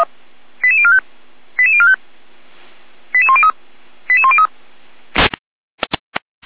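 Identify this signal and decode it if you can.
Five-tone selective call (ZVEI), heard as quick runs of five tones, tones of about 70 ms. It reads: 88045 88045 88103 88103
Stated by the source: ZVEI five-tone selective call 88385 received via SDR.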